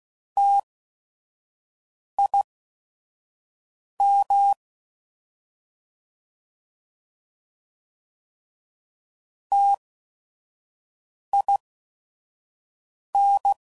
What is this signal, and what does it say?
Morse code, heard as a keyed tone, about 16 wpm, spaced as for 5 wpm, 800 Hz. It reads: TIM TIN